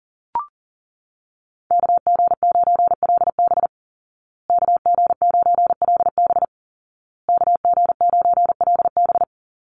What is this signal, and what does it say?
Morse code, heard as a keyed tone, about 40 wpm, 700 Hz. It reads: KG9LB KG9LB KG9LB
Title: Call Signs Repeated 2x - 40wpm